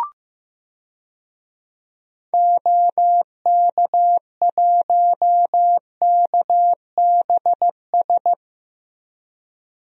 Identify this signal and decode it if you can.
Morse code, heard as a keyed tone, about 15 wpm, 700 Hz. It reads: OK1KBS
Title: DXCC-100 Call Signs - 6 Characters - Repeated 2x - 15wpm